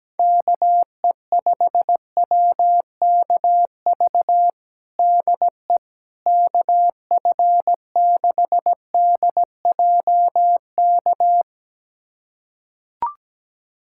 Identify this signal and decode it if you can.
Morse code, heard as a keyed tone, about 17 wpm, 700 Hz. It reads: KE5WKV DE KF6DJK
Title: USA Callsign DE Callsign - 6 Characters - 17wpm